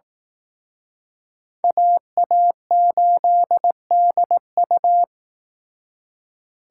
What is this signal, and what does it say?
Morse code, heard as a keyed tone, about 18 wpm, 700 Hz. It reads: AA8DU
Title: Speed Racing - USA 2x2 Call Signs 15wpm